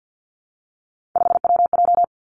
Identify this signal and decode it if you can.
Morse code, heard as a keyed tone, about 50 wpm, 700 Hz. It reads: 5WJ
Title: Three Challenging Letter-Number Combinations 50wpm